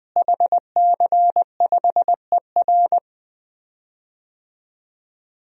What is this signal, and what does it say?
Morse code, heard as a keyed tone, about 20 wpm, 700 Hz. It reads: HC5ER
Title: DXCC-100 Call Signs - 5 Characters - Repeated 2x - 20wpm